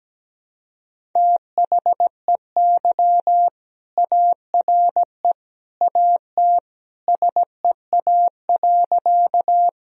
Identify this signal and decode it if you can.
Morse code, heard as a keyed tone, about 17 wpm, 700 Hz. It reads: THEY ARE AT SEA.
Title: Sentences from Top 1000 Words - Review Part 1 - 17wpm